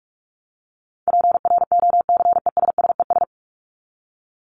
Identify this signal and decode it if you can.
Morse code, heard as a keyed tone, about 44 wpm, 700 Hz. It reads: PROCESSES
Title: ES:S Suffix - Top 100 Words with Base Words 45wpm